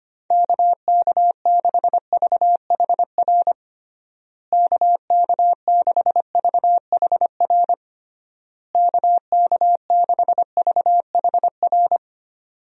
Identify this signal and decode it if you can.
Morse code, heard as a keyed tone, about 25 wpm, 700 Hz. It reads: KK6VHR KK6VHR KK6VHR
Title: USA Call Signs - 6 Characters - Repeated 3x - 25wpm